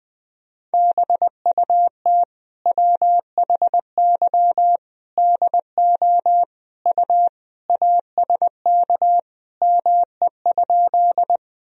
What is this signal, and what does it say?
Morse code, heard as a keyed tone, about 20 wpm, 700 Hz. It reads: BUT WHY DO U ASK ME?
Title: Sentences from Top 200 Words and Abbreviations 20wpm